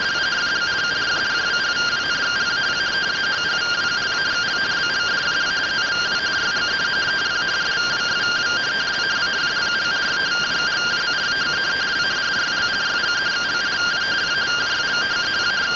Two-tone teleprinter signal, about 170 Hz shift.